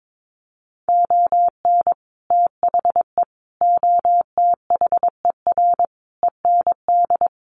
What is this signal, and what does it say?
Morse code, heard as a keyed tone, about 22 wpm, 700 Hz. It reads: ON THE OTHER END
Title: Sets of 5 Words Encore 22wpm